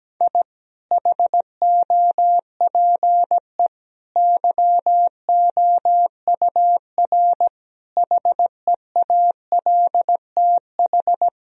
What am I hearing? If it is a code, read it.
Morse code, heard as a keyed tone, about 17 wpm, 700 Hz. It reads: I HOPE YOUR HEALTH
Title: Sentences from Top 1000 Words - Review Part 3 - 17wpm